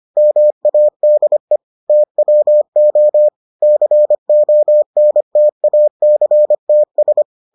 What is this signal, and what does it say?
Morse code, heard as a keyed tone, about 25 wpm, 600 Hz. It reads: MADE TWO CONTACTS